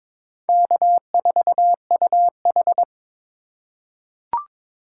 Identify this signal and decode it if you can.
Morse code, heard as a keyed tone, about 22 wpm, 700 Hz. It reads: K4UH
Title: USA Call Signs - 2x Character Spacing - 22wpm